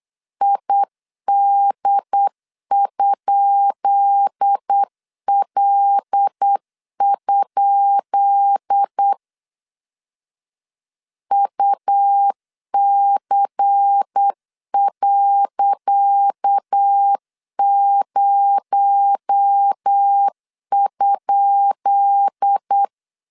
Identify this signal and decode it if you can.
Morse code, heard as a keyed tone, about 8 wpm, 800 Hz. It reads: ID?L? UC.0?
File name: LEZIONI / 10 SIGNS / 10C DOT 8wpm 3649